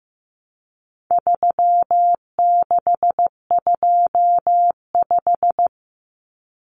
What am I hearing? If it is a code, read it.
Morse code, heard as a keyed tone, about 15 wpm, 700 Hz. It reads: 3625
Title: Common Cut Numbers - 4-Digit 15wpm